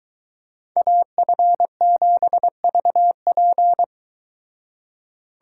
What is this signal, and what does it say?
Morse code, heard as a keyed tone, about 23 wpm, 700 Hz. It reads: AF7VP